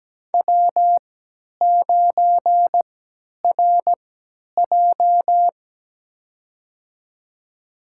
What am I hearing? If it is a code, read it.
Morse code, heard as a keyed tone, about 17 wpm, spaced as for 10 wpm, 700 Hz. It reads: W9RJ
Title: USA Call Signs - 4 Characters - 3x Character Spacing - 17wpm